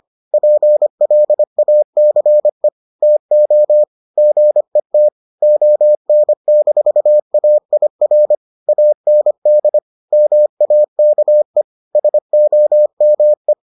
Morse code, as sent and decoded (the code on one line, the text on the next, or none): .--. .-.. .- -.-. / - --- / --. - / --- -. -....- .- .. .-. / .- -. -.. / -- .- -.- . / ... --- -- .
PLACE TO GET ON-AIR AND MAKE SOME